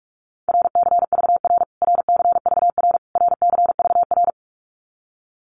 Morse code, heard as a keyed tone, about 44 wpm, 700 Hz. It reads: RCVR RCVR RCVR
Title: Abbreviations Repeated 3x - 45wpm